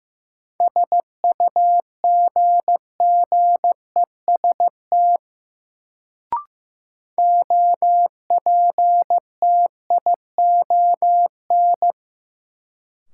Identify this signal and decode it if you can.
Morse code, heard as a keyed tone, about 16 wpm, 700 Hz. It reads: SUGGEST OPTION
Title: Sentences from Top 1300 Words - New Word Drill Alt - 15wpm